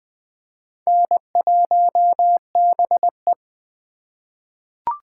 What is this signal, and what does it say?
Morse code, heard as a keyed tone, about 20 wpm, 700 Hz. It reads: N1BE
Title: USA Call Signs - 4 Characters - 4x Character Spacing - 20wpm